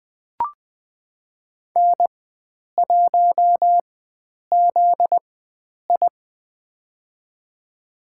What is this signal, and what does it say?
Morse code, heard as a keyed tone, about 20 wpm, spaced as for 9 wpm, 700 Hz. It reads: N1ZI
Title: USA Call Signs - 4x Character Spacing - 20wpm